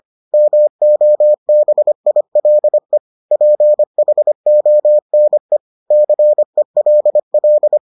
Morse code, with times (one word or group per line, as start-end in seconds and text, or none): 0.3-3.0 MOBILE
3.3-5.6 PHONE
5.9-7.8 CELL